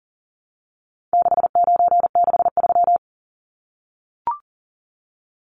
Morse code, heard as a keyed tone, about 40 wpm, 700 Hz. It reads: 6963
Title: Numbers - 4-Digit 40wpm